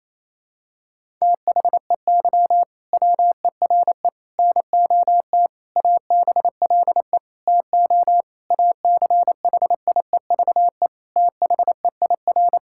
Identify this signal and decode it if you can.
Morse code, heard as a keyed tone, about 28 wpm, 700 Hz. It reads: THEY WERE NOT ABLE TO ACHIEVE THEIR
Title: Sentences from Top 1000 Words - Review Part 6 - 28wpm